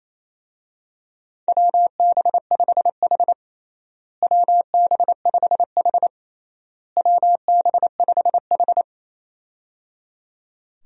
Morse code, heard as a keyed tone, about 28 wpm, 700 Hz. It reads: WB5H WB5H WB5H
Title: USA Call Signs - 4 Characters - Repeated 3x - 28wpm